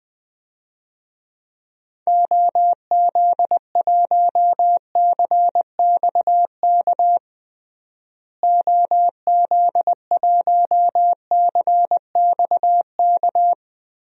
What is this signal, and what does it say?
Morse code, heard as a keyed tone, about 19 wpm, 700 Hz. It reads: OZ1CXK OZ1CXK
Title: DXCC-100 Call Signs - 6 Characters - Repeated 3x - 20wpm